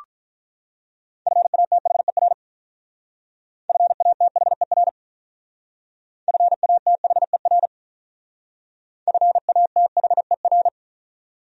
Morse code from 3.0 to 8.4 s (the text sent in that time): FATHER FATHER